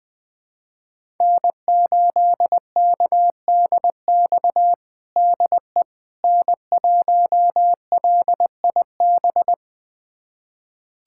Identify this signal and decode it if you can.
Morse code, heard as a keyed tone, about 20 wpm, 700 Hz. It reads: N8KDX DE N1LIB
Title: USA Callsign DE Callsign - 5 Characters - 20wpm